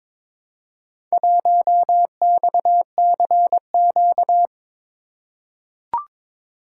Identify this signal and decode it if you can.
Morse code, heard as a keyed tone, about 22 wpm, 700 Hz. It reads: 1XCQ
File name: practice / 22wpm / Four Letter-Number Combinations - 22wpm